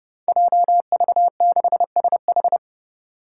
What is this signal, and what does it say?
Morse code, heard as a keyed tone, about 30 wpm, 700 Hz. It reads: JV6SH
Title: Five Challenging Letter-Number Combinations 30wpm